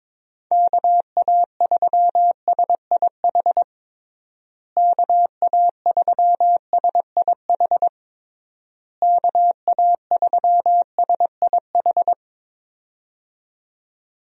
Morse code, heard as a keyed tone, about 22 wpm, 700 Hz. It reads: KA3SIH KA3SIH KA3SIH
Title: USA Call Signs - 6 Characters - Repeated 3x - 22wpm